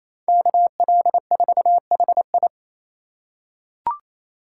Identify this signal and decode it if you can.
Morse code, heard as a keyed tone, about 28 wpm, 700 Hz. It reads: KL4HI
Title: USA Call Signs - 5 Characters - 2x Character Spacing - 28wpm